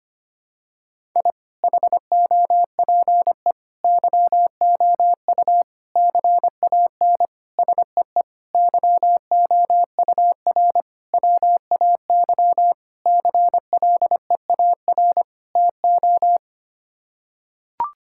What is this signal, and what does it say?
Morse code, heard as a keyed tone, about 25 wpm, 700 Hz. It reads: I HOPE YOU CAN SEE YOUR WAY CLEAR TO
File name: Sets of 9 Words - 25wpm